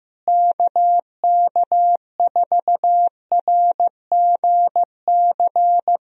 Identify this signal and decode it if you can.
Morse code, heard as a keyed tone, about 15 wpm, 700 Hz. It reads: KK4RGC
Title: USA Call Signs - 6 Characters - Repeated 3x - 15wpm